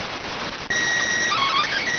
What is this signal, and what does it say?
SSTV (slow-scan television), PD 290.